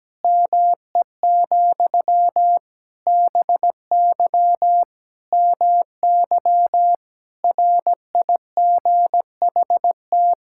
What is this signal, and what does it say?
Morse code, heard as a keyed tone, about 17 wpm, 700 Hz. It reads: ME, BY MY RIGHT
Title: Sentences from Top 1000 Words - Review Part 6 - 17wpm